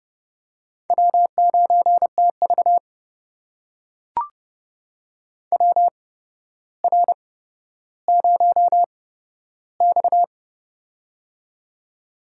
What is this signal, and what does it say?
Morse code, heard as a keyed tone, about 30 wpm, spaced as for 13 wpm, 700 Hz. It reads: W9TV WR0X
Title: USA Call Signs - 4 Characters - 8x Character Spacing - 30wpm